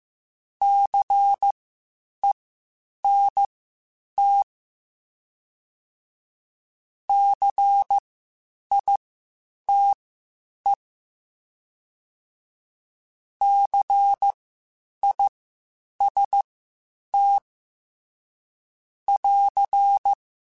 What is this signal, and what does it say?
Morse code, heard as a keyed tone, about 15 wpm, spaced as for 8 wpm, 800 Hz. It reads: CENT CITE CIST <AR>
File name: Spaced Alphas / Spaced Alphas mp3s / Cs sp @13